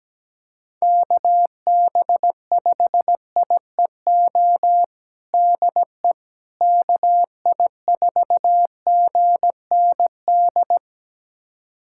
Morse code, heard as a keyed tone, about 17 wpm, 700 Hz. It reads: KB5IEO DE KI4GND
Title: USA Callsign DE Callsign 17wpm